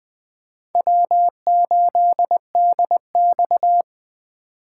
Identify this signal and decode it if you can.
Morse code, heard as a keyed tone, about 20 wpm, 700 Hz. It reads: W8DX
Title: Speed Racing - USA 1x2 Call Signs 17wpm